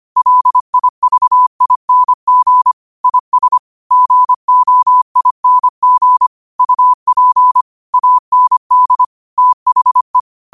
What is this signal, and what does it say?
Morse code, heard as a keyed tone, about 25 wpm, 1000 Hz. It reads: LIVING IS GOING UP AND THE
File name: qotdfiles / 25wpm